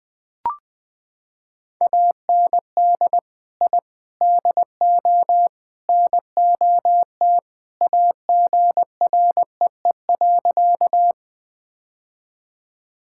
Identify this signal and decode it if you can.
Morse code, heard as a keyed tone, about 20 wpm, 700 Hz. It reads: AND I DO NOT AGREE.
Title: Sentences with 4-Word Phrases - 20wpm